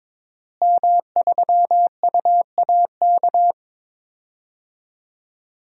Morse code, heard as a keyed tone, about 22 wpm, 700 Hz. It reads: M3UAK
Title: DXCC-100 Call Signs - 5 Characters - Repeated 2x - 22wpm